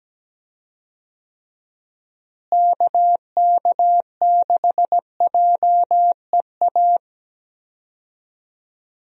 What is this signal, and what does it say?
Morse code, heard as a keyed tone, about 17 wpm, 700 Hz. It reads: KK6JEA
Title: USA Call Signs - 6 Characters - Repeated 2x - 17wpm